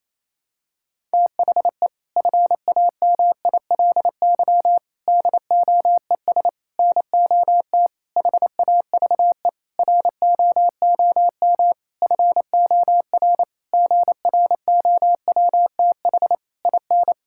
Morse code, heard as a keyed tone, about 28 wpm, 700 Hz. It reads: THE FAMILY DOES NOT HAVE ROOM FOR GROWTH IN
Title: Sentences from Top 1000 Words - Review Part 5 - 28wpm